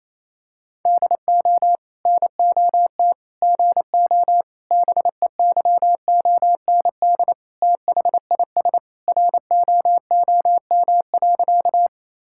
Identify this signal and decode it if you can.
Morse code, heard as a keyed tone, about 28 wpm, 700 Hz. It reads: DO NOT GO BEYOND THIS ROOM.
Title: Sentences from Top 1000 Words - Review Part 6 - 28wpm